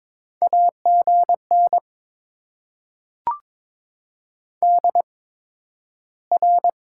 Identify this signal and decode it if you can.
Morse code, heard as a keyed tone, about 23 wpm, spaced as for 12 wpm, 690 Hz. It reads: AGN DR
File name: Top 90 QSO Elements - 8x Character Spacing - 22wpm